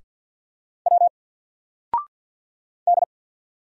Morse code, heard as a keyed tone, about 48 wpm, 700 Hz. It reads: W D
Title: Single Letter-Number Alt 50wpm